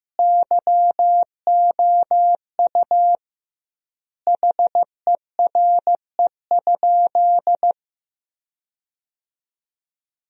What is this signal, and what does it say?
Morse code, heard as a keyed tone, about 15 wpm, 700 Hz. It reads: YOU HERE?
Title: 2x Word-Spacing - Sentences from Top 500 Words - 15wpm